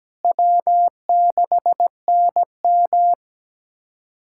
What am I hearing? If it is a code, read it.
Morse code, heard as a keyed tone, about 17 wpm, 700 Hz. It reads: W6NM
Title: USA Callsign DE Callsign - 4 Characters - 17wpm